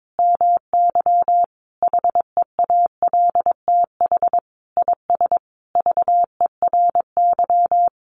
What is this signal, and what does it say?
Morse code, heard as a keyed tone, about 22 wpm, 700 Hz. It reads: MY HEALTH IS VERY
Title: Sentences from Top 1200 Words 22wpm